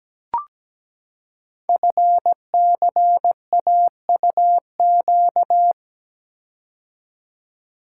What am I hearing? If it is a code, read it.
Morse code, heard as a keyed tone, about 17 wpm, 700 Hz. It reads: FCAUQ
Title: Code Groups - 5 Letters - 17wpm